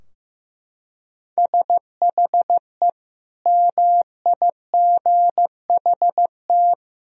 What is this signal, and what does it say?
Morse code, heard as a keyed tone, about 15 wpm, 700 Hz. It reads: SHE MIGHT
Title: Sentences from Top 1000 Words - Review Part 4 - 15wpm